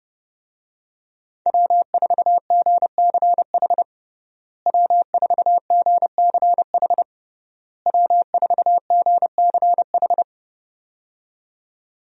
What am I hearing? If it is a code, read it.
Morse code, heard as a keyed tone, about 32 wpm, 700 Hz. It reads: W4GCH W4GCH W4GCH
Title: USA Call Signs - 5 Characters - Repeated 3x - 30wpm